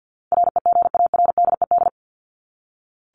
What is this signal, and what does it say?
Morse code, heard as a keyed tone, about 48 wpm, 700 Hz. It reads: REGARDED
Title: ED Suffix - Top 100 Words with Base Words 50wpm